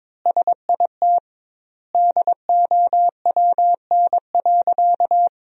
Morse code, heard as a keyed tone, about 22 wpm, 700 Hz. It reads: SIT DOWN.
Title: 2x Word-Spacing - Sentences from Top 500 Words - 22wpm